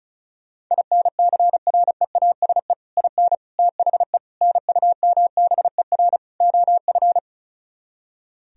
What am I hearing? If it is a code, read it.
Morse code, heard as a keyed tone, about 46 wpm, 700 Hz. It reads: INCREASE IN THE NUMBER OF